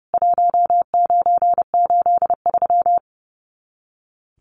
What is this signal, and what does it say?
Morse code, heard as a keyed tone, about 30 wpm, 700 Hz. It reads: 1983
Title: Common QSO Phrases 30wpm